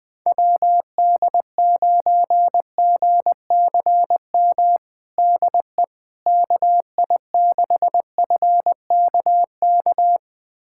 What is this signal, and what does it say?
Morse code, heard as a keyed tone, about 21 wpm, 700 Hz. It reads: WD9GCM DE KI6FKK
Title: USA Callsign DE Callsign - 6 Characters - 20wpm